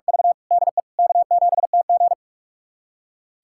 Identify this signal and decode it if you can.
Morse code, heard as a keyed tone, about 46 wpm, 700 Hz. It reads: V DE K7TG
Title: USA Callsign DE Callsign - 4 Characters - 45wpm